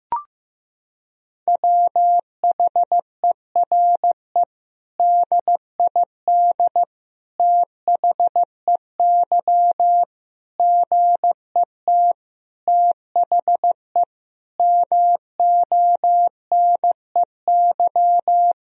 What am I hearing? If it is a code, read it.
Morse code, heard as a keyed tone, about 16 wpm, 700 Hz. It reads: WHERE DID THEY GET THE MONEY
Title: Sentences from Top 400 Words and Abbreviations 15wpm